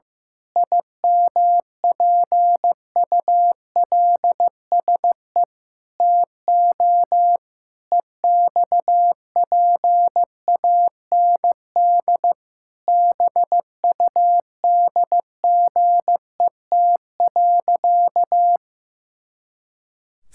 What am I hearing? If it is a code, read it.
Morse code, heard as a keyed tone, about 15 wpm, 700 Hz. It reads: IMPULSE TO EXPAND BUDGET.
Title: Pangram Challenge v2 - 15wpm